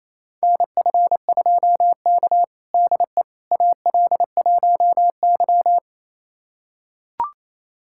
Morse code, heard as a keyed tone, about 28 wpm, 700 Hz. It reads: NF2K DE AL1Y
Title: USA Callsign DE Callsign - 4 Characters - 28wpm